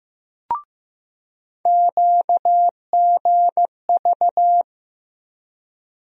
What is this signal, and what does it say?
Morse code, heard as a keyed tone, about 15 wpm, 700 Hz. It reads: QGV
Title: USA Call Sign Postfixes - 3 Letters 15wpm